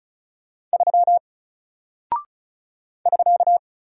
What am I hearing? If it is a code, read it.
Morse code, heard as a keyed tone, about 36 wpm, 700 Hz. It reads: 3 <SK>
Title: Single Character (Letter:Number:Punctuation:Prosign) Alt 35wpm